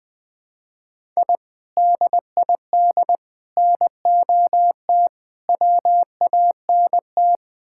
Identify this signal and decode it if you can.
Morse code, heard as a keyed tone, about 20 wpm, 690 Hz. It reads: I DID NOT WANT
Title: Build A Phrase - 4-Word Phrases - 20wpm